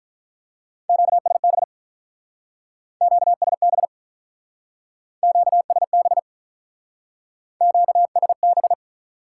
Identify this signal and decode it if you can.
Morse code, heard as a keyed tone, about 47 wpm, 700 Hz. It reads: QSB QSB QSB QSB